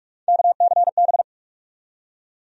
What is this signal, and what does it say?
Morse code, heard as a keyed tone, about 44 wpm, 700 Hz. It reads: KCB
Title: USA Call Sign Postfixes - 3 Letters 45wpm